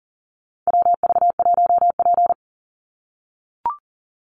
Morse code, heard as a keyed tone, about 40 wpm, 700 Hz. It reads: WV1P
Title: USA Call Signs 40wpm